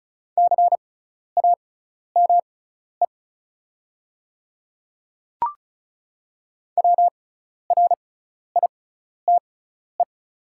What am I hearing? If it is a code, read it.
Morse code, heard as a keyed tone, about 34 wpm, spaced as for 12 wpm, 700 Hz. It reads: CAME WRITE
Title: 6x Character Spacing - Top 500 Words - 35wpm